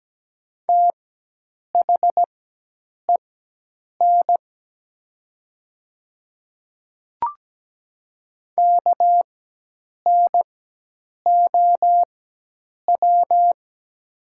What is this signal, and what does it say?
Morse code, heard as a keyed tone, about 17 wpm, spaced as for 8 wpm, 700 Hz. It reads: THEN KNOW
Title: 4x Character Spacing - Top 200 Words - 17wpm